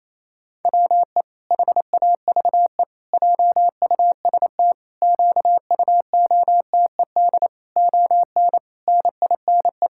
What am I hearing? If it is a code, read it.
Morse code, heard as a keyed tone, about 28 wpm, 700 Hz. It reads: WE HAVE JUST QUOTED ON NINE
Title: Pangram Challenge v2 - 28wpm